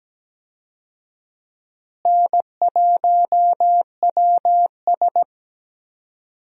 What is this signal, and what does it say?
Morse code, heard as a keyed tone, about 17 wpm, 700 Hz. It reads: N1WS
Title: USA Call Signs - 4 Characters - Repeated 3x - 17wpm